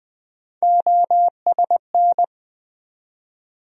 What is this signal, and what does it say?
Morse code, heard as a keyed tone, about 21 wpm, 700 Hz. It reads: OSN
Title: USA Call Sign Postfixes - 3 Letters 20wpm